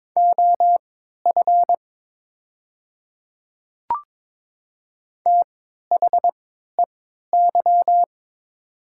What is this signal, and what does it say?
Morse code, heard as a keyed tone, about 23 wpm, spaced as for 13 wpm, 700 Hz. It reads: OF THEY
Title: 3x Character Spacing - Top 100 Words - 22wpm